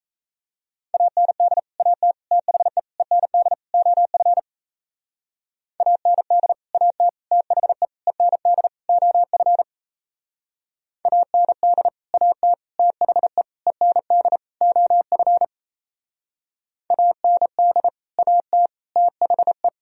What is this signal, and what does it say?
Morse code, heard as a keyed tone, about 35 wpm, 700 Hz. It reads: AND AT THE END OF AND AT THE END OF AND AT THE END OF AND AT THE
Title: Speed-Racing - Sets of 5 Words 28wpm